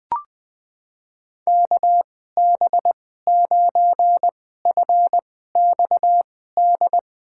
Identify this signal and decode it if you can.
Morse code, heard as a keyed tone, about 21 wpm, spaced as for 15 wpm, 700 Hz. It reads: KB9FXD
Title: USA Call Signs - 6 Characters - 2x Character Spacing - 20wpm